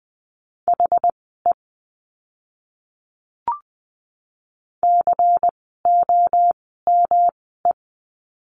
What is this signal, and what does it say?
Morse code, heard as a keyed tone, about 20 wpm, spaced as for 14 wpm, 700 Hz. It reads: HE COME
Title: 146 Lesson 24f - Letter y - Words 20wpm